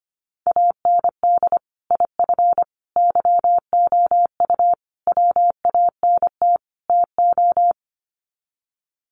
Morse code, heard as a keyed tone, about 25 wpm, 700 Hz. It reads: AND IF YOU WANT TO
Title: Sets of 5 Words Encore 25wpm